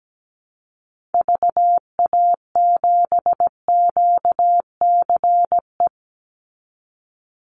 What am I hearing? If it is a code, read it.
Morse code, heard as a keyed tone, about 17 wpm, 700 Hz. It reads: VA7QCE
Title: Call Signs Repeated 3x - 17wpm